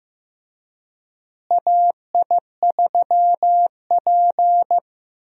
Morse code, heard as a keyed tone, about 15 wpm, 700 Hz. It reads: AI3P
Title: USA Call Signs - 4 Characters - Repeated 3x - 15wpm